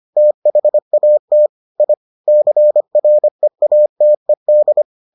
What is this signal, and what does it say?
Morse code, heard as a keyed tone, about 25 wpm, 600 Hz. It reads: THAT I CREATED